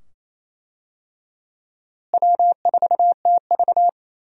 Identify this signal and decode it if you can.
Morse code, heard as a keyed tone, about 28 wpm, 700 Hz. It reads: W4TV
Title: USA Call Signs - 4 Characters - Repeated 2x - 28wpm